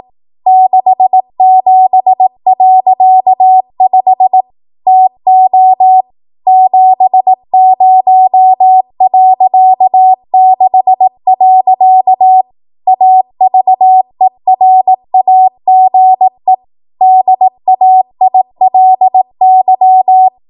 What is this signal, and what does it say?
Morse code, heard as a keyed tone, about 18 wpm, 750 Hz. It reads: 67.5 TO 70.6. AVERAGE DAILY
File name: Archive / 18 WPM / 18WPM